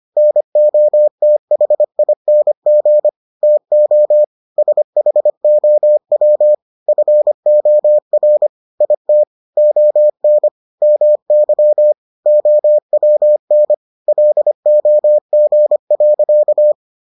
Morse code, heard as a keyed tone, about 25 wpm, 600 Hz. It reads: NOTHING TO SHOW FOR IT ON MY OWN LOG.